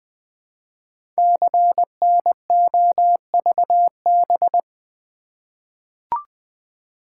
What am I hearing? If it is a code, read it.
Morse code, heard as a keyed tone, about 20 wpm, 700 Hz. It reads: CNOVB
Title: Code Groups - 5 Letters - 20wpm